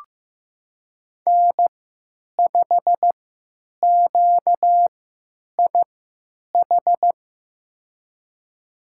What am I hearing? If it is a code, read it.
Morse code, heard as a keyed tone, about 15 wpm, spaced as for 9 wpm, 700 Hz. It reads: N5QIH
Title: USA Call Signs - 5 Characters - 3x Character Spacing - 15wpm